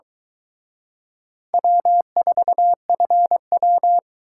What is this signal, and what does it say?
Morse code, heard as a keyed tone, about 22 wpm, 700 Hz. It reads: W4FW